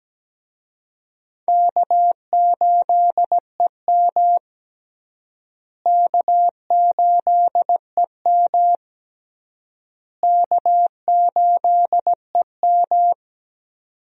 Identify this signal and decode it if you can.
Morse code, heard as a keyed tone, about 17 wpm, 700 Hz. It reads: K8EM K8EM K8EM
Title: USA Call Signs - 4 Characters - Repeated 3x - 17wpm